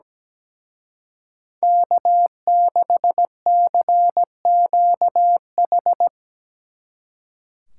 Morse code, heard as a keyed tone, about 20 wpm, 700 Hz. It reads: K6CQH
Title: Speed Racing - USA 1x3 Call Signs 17wpm